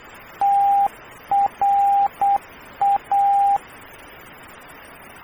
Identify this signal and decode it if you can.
Morse code, heard as a keyed tone, about 8 wpm, 800 Hz, with radio noise. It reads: TRA
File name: radio tv historisch / AM Sender / TRA